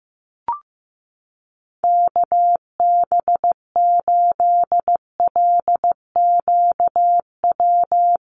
Morse code, heard as a keyed tone, about 15 wpm, 700 Hz. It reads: KB8LQW
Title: USA Call Signs 15wpm